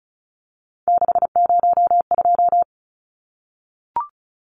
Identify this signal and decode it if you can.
Morse code, heard as a keyed tone, about 34 wpm, 700 Hz. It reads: 602
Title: Numbers - 3-Digit 35wpm